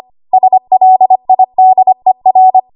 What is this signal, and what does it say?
Morse code, heard as a keyed tone, about 25 wpm, 750 Hz. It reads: SLIDER
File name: Archive / 25 WPM / 25WPM